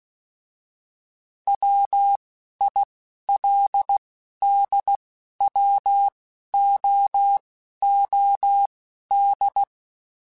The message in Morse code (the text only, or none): WILDWOOD